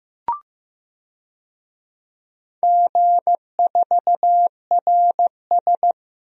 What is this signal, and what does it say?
Morse code, heard as a keyed tone, about 15 wpm, 700 Hz. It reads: G4RS
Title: DXCC-100 Call Signs - 4 Characters - Repeated 2x - 15wpm